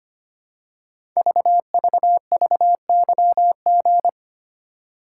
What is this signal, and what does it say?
Morse code, heard as a keyed tone, about 25 wpm, 700 Hz. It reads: VVVYG
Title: Code Groups - 5 Letters - 25wpm